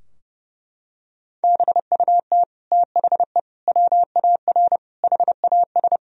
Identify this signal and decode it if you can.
Morse code, heard as a keyed tone, about 32 wpm, 700 Hz. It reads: BUT THE WAR HAS
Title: Sentences from Top 1000 Words - Review Part 2 - 30wpm